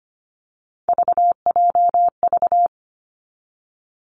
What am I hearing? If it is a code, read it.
Morse code, heard as a keyed tone, about 25 wpm, 700 Hz. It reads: VJV
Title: Three Challenging Letter-Number Combinations 25wpm